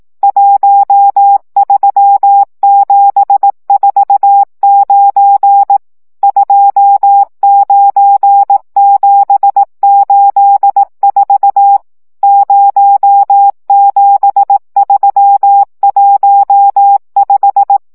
Morse code, heard as a keyed tone, about 18 wpm, 800 Hz. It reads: 13749 29784 07315
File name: morze vizsga 2011-3 15